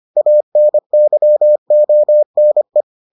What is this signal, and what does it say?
Morse code, heard as a keyed tone, about 25 wpm, 600 Hz. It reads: ANYONE